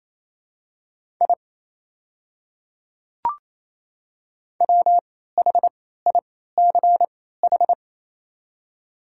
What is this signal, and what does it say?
Morse code, heard as a keyed tone, about 28 wpm, spaced as for 16 wpm, 700 Hz. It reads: I WHICH
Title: 3x Character Spacing - Top 100 Words - 28wpm